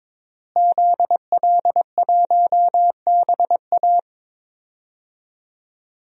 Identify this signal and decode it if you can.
Morse code, heard as a keyed tone, about 22 wpm, 700 Hz. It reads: ZL1BA
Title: DXCC-100 Call Signs - 5 Characters - Repeated 2x - 22wpm